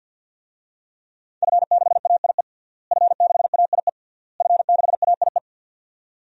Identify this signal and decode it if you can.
Morse code, heard as a keyed tone, about 51 wpm, 700 Hz. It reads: F6AIE F6AIE F6AIE
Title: DXCC-100 Call Signs - 5 Characters - Repeated 3x - 50wpm